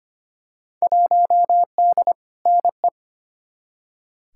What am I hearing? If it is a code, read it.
Morse code, heard as a keyed tone, about 25 wpm, 700 Hz. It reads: 1D NE
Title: Field Day Exchanges - 25wpm